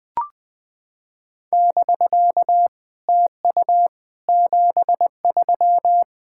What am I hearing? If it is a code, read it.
Morse code, heard as a keyed tone, about 20 wpm, 700 Hz. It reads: <BK> TU 73
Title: SOTA QSO Practice 20wpm